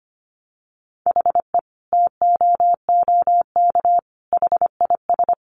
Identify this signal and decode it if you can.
Morse code, heard as a keyed tone, about 25 wpm, 700 Hz. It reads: HE TOOK HIS
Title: Sentences from Top 900 Words 25wpm